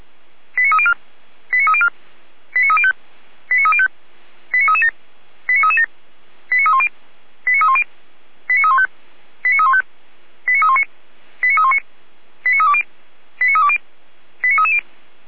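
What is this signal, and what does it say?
Five-tone selective call (ZVEI), heard as quick runs of five tones, tones of about 70 ms. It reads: 89304 89304 89305 89305 89337 89337 89310 89310 89315 89315 89319 89319 89320 89320 89330
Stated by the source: Received on an SDR, ZVEI five-tone code 89280.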